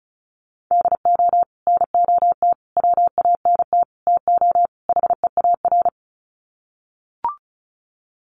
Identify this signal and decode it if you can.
Morse code, heard as a keyed tone, about 34 wpm, 700 Hz. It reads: DO NOT WANT TO HEAR
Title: Sets of 5 Words 35wpm